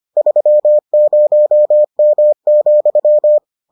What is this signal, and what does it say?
Morse code, heard as a keyed tone, about 25 wpm, 600 Hz. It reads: 30M,